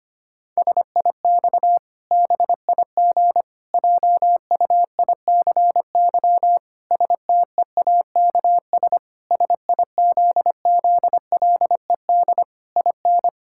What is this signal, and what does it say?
Morse code, heard as a keyed tone, about 25 wpm, 700 Hz. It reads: SIX BIG JUICY STEAKS SIZZLED IN
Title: Pangram Challenge v2 - 25wpm